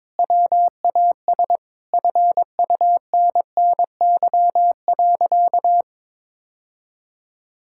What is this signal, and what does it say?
Morse code, heard as a keyed tone, about 22 wpm, 700 Hz. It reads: WAS FUNNY.
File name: practice / 22wpm / Sentences with 2-Word Phrases - 22wpm